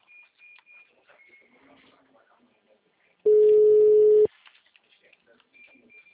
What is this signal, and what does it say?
Ringback tone (Europe, 425 Hz), with speech.